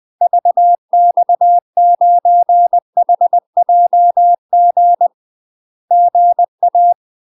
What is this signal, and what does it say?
Morse code, heard as a keyed tone, about 20 wpm, 700 Hz. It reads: VX9HJG GA
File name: UK MASTER QSOs TRAINING/20WPM